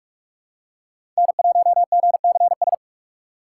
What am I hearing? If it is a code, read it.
Morse code, heard as a keyed tone, about 46 wpm, 700 Hz. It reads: N1GCS
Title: USA Call Signs - 6x Character Spacing - 45wpm